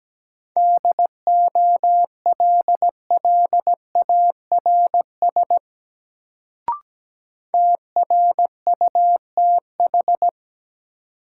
Morse code, heard as a keyed tone, about 17 wpm, 700 Hz. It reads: DOLLARS TRUTH
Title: Sets of 5 Words - New Word Drill - 17wpm